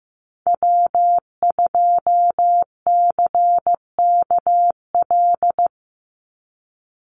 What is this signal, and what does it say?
Morse code, heard as a keyed tone, about 15 wpm, 700 Hz. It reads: W2CKL
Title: NA SOTA Activator Callsigns 15wpm